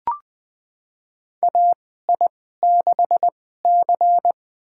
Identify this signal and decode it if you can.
Morse code, heard as a keyed tone, about 20 wpm, spaced as for 15 wpm, 700 Hz. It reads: AI6C
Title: USA Call Signs - 4 Characters - 2x Character Spacing - 20wpm